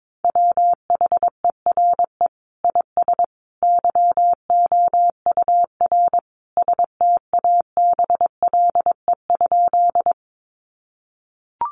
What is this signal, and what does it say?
Morse code, heard as a keyed tone, about 22 wpm, 700 Hz. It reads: WHERE IS YOUR STABLE?
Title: Sentences from Top 1500 Words 22wpm